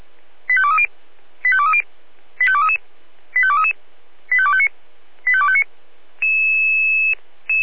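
Five-tone selective call (ZVEI), heard as quick runs of five tones, tones of about 80 ms. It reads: 85230 85230 85233 85233 85259 85259
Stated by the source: ZVEI selective call received by SDR.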